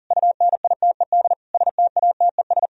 Morse code, heard as a keyed tone, about 51 wpm, 700 Hz. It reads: UNITED STATES